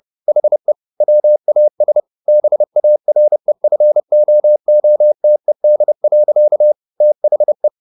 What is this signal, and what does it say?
Morse code, heard as a keyed tone, about 30 wpm, 600 Hz. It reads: HE WAS BAREFOOTED. THE
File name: the old man and the sea / the old man and the sea 1 30wpm